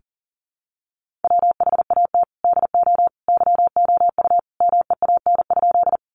Morse code, heard as a keyed tone, about 40 wpm, 700 Hz. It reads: WHAT DO YOU MEAN?
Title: Sentences from Top 200 Words 40wpm